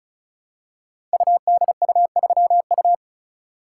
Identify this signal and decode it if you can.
Morse code, heard as a keyed tone, about 36 wpm, 700 Hz. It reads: UDU3U
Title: Code Groups - 5 Letter:Numbers - 35wpm